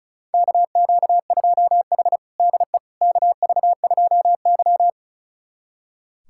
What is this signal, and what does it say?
Morse code, heard as a keyed tone, about 36 wpm, 700 Hz. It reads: KQ2H DE KV2Y
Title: USA Callsign DE Callsign - 4 Characters - 35wpm